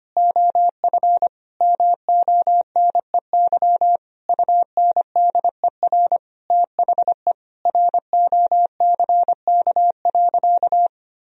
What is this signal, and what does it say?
Morse code, heard as a keyed tone, about 25 wpm, 700 Hz. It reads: OF MONEY UNDER THE ROCK.
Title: Sentences from Top 1000 Words - Review Part 2 - 25wpm